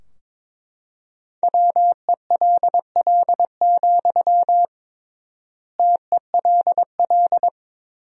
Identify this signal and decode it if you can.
Morse code, heard as a keyed tone, about 21 wpm, 700 Hz. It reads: WELL, TELL
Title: 3x Word-Spacing - Sentences from Top 200 Words - 22wpm